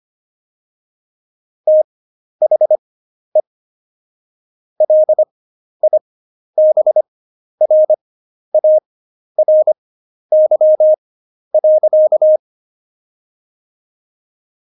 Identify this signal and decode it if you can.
Morse code, heard as a keyed tone, about 25 wpm, spaced as for 11 wpm, 620 Hz. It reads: THE LIBRARY.
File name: Jokes-25-char-10-eff